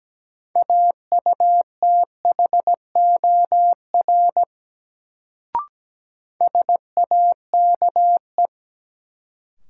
Morse code, heard as a keyed tone, about 17 wpm, 700 Hz. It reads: AUTHOR SAKE
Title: Sets of 4 Words - New Word Drill - 17wpm